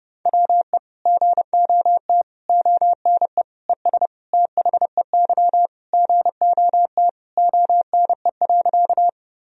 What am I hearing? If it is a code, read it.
Morse code, heard as a keyed tone, about 30 wpm, 700 Hz. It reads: WE GOT ONE ES THEY GOT ONE.
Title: Sentences from Top 300 Words and Abbreviations 30wpm